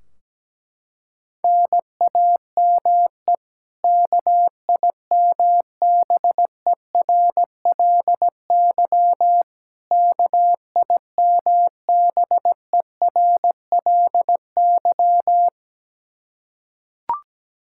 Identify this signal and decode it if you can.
Morse code, heard as a keyed tone, about 17 wpm, 700 Hz. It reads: NAME KIMBERLY KIMBERLY
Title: Practice QSO - Exchange 1 - Beginning - 17wpm